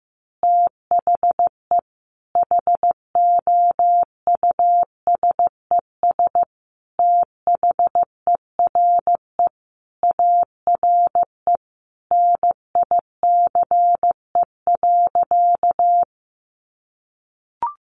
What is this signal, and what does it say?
Morse code, heard as a keyed tone, about 15 wpm, 700 Hz. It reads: THE HOUSES THERE ARE NICE.
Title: Easy Phrases 15wpm